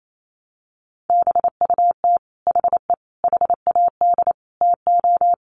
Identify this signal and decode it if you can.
Morse code, heard as a keyed tone, about 28 wpm, 700 Hz. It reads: BUT HE HAD TO
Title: Sentences from Top 100 Words 28wpm